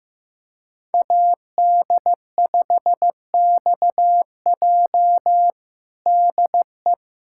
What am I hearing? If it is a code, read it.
Morse code, heard as a keyed tone, about 15 wpm, 700 Hz. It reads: AD5XJ DE
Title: USA Callsign DE Callsign 15wpm